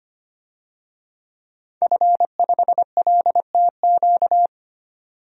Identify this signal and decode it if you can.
Morse code, heard as a keyed tone, about 25 wpm, 700 Hz. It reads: F5LTQ
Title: DXCC-100 Call Signs - 5 Characters - 25wpm